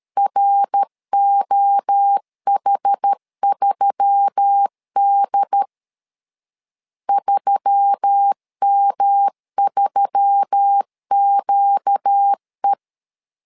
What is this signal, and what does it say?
Morse code, heard as a keyed tone, about 13 wpm, 800 Hz. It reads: ROH3D 3M3QE
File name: LEZIONI / 4 Q1S3 12wpm 2023